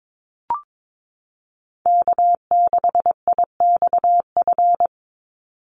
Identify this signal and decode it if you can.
Morse code, heard as a keyed tone, about 22 wpm, 700 Hz. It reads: K6IXF
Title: USA Call Signs 22wpm